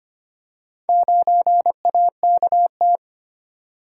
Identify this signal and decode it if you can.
Morse code, heard as a keyed tone, about 25 wpm, 700 Hz. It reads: 9AKT
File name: practice / 25wpm / Four Letter-Number Combinations - 25wpm